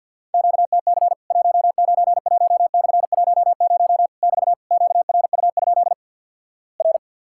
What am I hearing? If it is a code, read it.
Morse code, heard as a keyed tone, about 50 wpm, 700 Hz. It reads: QTC 191/10 <BT> QRU? R
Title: Worked All Europe DX Contest - QTC Exchange with Repeats - 50wpm